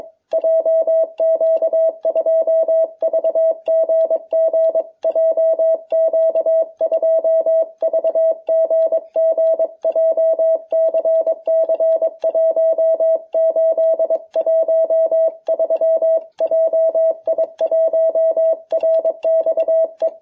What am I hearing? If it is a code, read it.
Morse code, heard as a keyed tone, about 23 wpm, 660 Hz. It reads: JQ2VGG JQ2VGG JCC 1813 JI1RXE